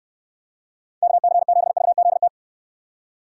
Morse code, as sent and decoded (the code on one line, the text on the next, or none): -.- -.-. --... ...- --.. -
KC7VZT